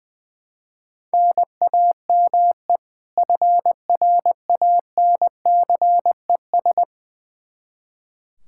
Morse code, heard as a keyed tone, about 20 wpm, 700 Hz. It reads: NAME FRANCES
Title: Practice QSO - Exchange 1 - Beginning - 20wpm